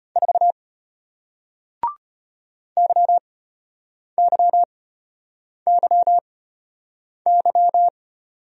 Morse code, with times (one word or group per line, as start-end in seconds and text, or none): 0.2-0.5 4
2.8-7.9 YYYY